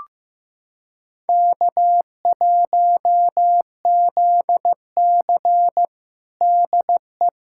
Morse code, heard as a keyed tone, about 16 wpm, 700 Hz. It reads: K1ZC DE
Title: USA Callsign DE Callsign - 4 Characters - 15wpm